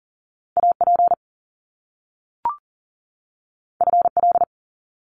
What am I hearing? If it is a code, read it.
Morse code, heard as a keyed tone, about 40 wpm, 700 Hz. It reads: AP FL
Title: Top Two Letter Combinations in Words 40wpm